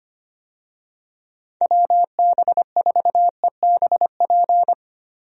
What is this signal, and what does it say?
Morse code, heard as a keyed tone, about 25 wpm, 700 Hz. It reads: WB4EBP
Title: USA Call Signs - 6 Characters - 25wpm